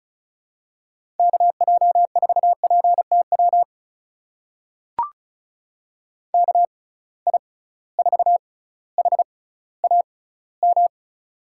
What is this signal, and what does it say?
Morse code, heard as a keyed tone, about 36 wpm, spaced as for 18 wpm, 700 Hz. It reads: KJ4PTW KI4HAM
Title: USA Call Signs - 6x Character Spacing - 35wpm